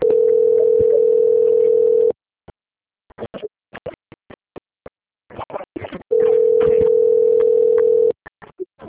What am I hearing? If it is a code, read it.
Ringback tone (North America), 440 + 480 Hz.